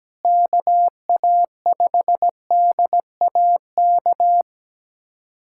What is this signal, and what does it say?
Morse code, heard as a keyed tone, about 17 wpm, 700 Hz. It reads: KA5DAK
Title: USA Callsign DE Callsign 17wpm